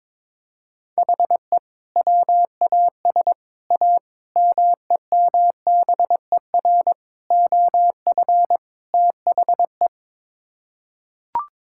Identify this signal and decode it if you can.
Morse code, heard as a keyed tone, about 22 wpm, 700 Hz. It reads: HE WAS A MEMBER OF THE
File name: practice / 22wpm / Sets of 6 Words - 22wpm